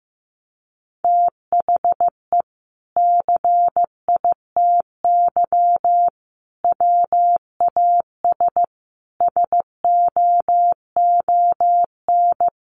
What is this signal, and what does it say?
Morse code, heard as a keyed tone, about 15 wpm, 700 Hz. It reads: THE CITY WAS SOON
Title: Sentences from Top 500 Words 15wpm